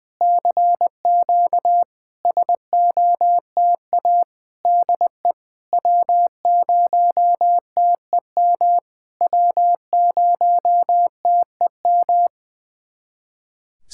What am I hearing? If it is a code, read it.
Morse code, heard as a keyed tone, about 20 wpm, 690 Hz. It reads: CQ SOTA DE W0TEM W0TEM
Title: SOTA QSO Practice 20wpm